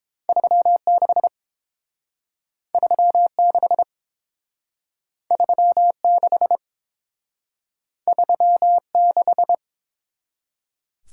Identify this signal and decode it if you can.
Morse code, heard as a keyed tone, about 29 wpm, 700 Hz. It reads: 36 36 36 36